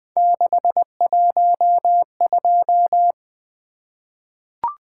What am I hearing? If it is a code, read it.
Morse code, heard as a keyed tone, about 20 wpm, 700 Hz. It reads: 612
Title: Three Letter-Number Combinations - 20wpm